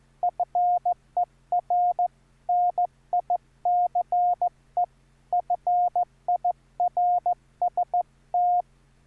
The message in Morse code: FER NICE FIRST